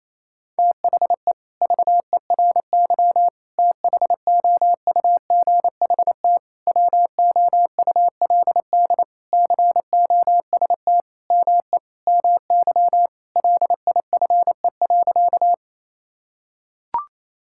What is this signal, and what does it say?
Morse code, heard as a keyed tone, about 27 wpm, 700 Hz. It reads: THE VERY THOUGHT WOULD COST ME MY LIFE.
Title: Sentences from Top 1000 Words - Review Part 5 - 28wpm